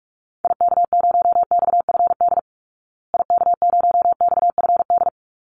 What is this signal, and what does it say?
Morse code, heard as a keyed tone, about 44 wpm, 700 Hz. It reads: IK0XFD IK0XFD
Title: Call Signs Repeated 3x - 45wpm